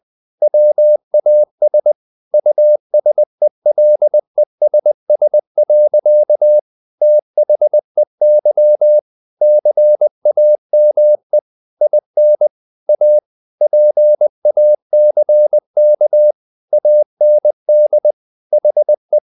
Morse code, heard as a keyed tone, about 20 wpm, 600 Hz. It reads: WAS USELESS. THEY CAME IN A PACK AND HE
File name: the old man and the sea part 2 20wpm